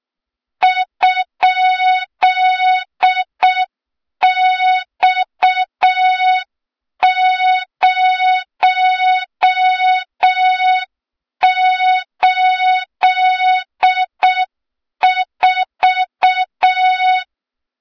Morse code, heard as a keyed tone, about 6 wpm, 3700 Hz, with noise. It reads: ?X084
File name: Morze-HA2MN / hang / 23 GYAKORLAT